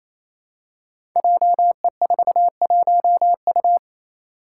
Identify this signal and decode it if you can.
Morse code, heard as a keyed tone, about 27 wpm, 700 Hz. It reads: JE41U
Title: Five Challenging Letter-Number Combinations 28wpm